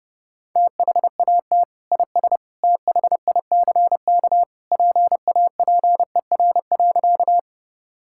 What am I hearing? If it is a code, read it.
Morse code, heard as a keyed tone, about 32 wpm, 700 Hz. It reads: THAT IS THICK PAPER.
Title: Sentences from Top 1600 Words 30wpm